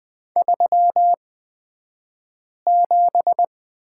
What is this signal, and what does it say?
Morse code, heard as a keyed tone, about 20 wpm, spaced as for 5 wpm, 700 Hz. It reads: 37
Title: 235 Lesson 38d - Letter q - Two Characters 20wpm